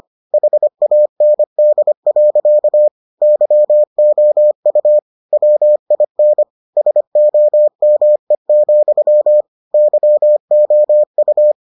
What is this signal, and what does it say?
Morse code, heard as a keyed tone, about 25 wpm, 600 Hz. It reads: HAND. YOU WIN SOME, YOU